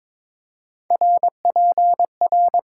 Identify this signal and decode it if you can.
Morse code, heard as a keyed tone, about 22 wpm, 700 Hz. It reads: RPR
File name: Top 90 QSO Elements - 8x Character Spacing - 22wpm